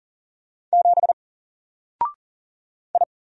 Morse code, heard as a keyed tone, about 40 wpm, 700 Hz. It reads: 7 I
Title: Single Letter-Number 40wpm